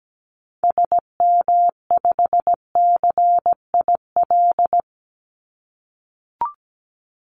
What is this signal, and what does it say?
Morse code, heard as a keyed tone, about 17 wpm, 700 Hz. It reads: SM5CIL
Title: Call Signs - 17wpm